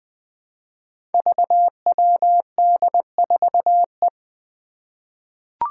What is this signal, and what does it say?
Morse code, heard as a keyed tone, about 20 wpm, 700 Hz. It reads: VWD4E
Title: Five Challenging Letter-Number Combinations 20wpm